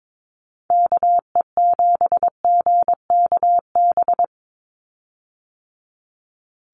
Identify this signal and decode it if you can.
Morse code, heard as a keyed tone, about 22 wpm, 700 Hz. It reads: KE7GKB
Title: USA Call Signs 22wpm